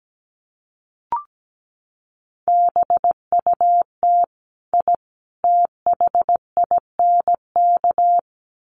Morse code, heard as a keyed tone, about 17 wpm, 700 Hz. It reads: BUT I THINK
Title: Sets of 3 Words 17wpm